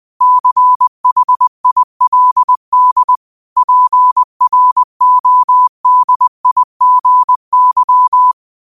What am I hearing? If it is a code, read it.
Morse code, heard as a keyed tone, about 20 wpm, 1000 Hz. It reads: CHILD PRODIGY
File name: qotdfiles / 20wpm